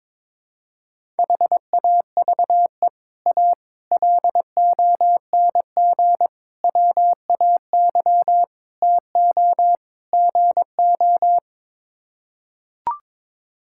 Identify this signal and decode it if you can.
Morse code, heard as a keyed tone, about 22 wpm, 700 Hz. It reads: HAVE A LONG WAY TO GO
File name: practice / 22wpm / Sets of 6 Words - 22wpm